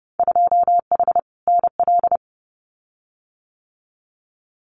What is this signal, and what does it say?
Morse code, heard as a keyed tone, about 30 wpm, 700 Hz. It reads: 2H NL
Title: Winter Field Day Exchanges Encore 30wpm